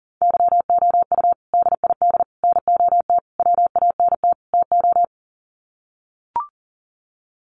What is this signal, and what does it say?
Morse code, heard as a keyed tone, about 40 wpm, 700 Hz. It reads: YOU DID NOT WANT TO
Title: Sets of 5 Words Encore2 40wpm